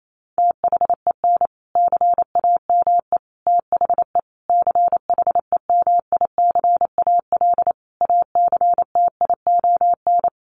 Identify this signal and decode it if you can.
Morse code, heard as a keyed tone, about 28 wpm, 700 Hz. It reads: THEN CAME THE CHEMICAL ACTION
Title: Sentences from Top 1200 Words 28wpm